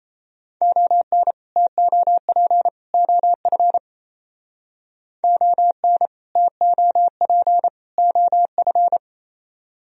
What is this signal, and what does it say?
Morse code, heard as a keyed tone, about 34 wpm, 700 Hz. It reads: ON TOP OF ON TOP OF
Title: Speed-Racing - Sets of 3 Words 28wpm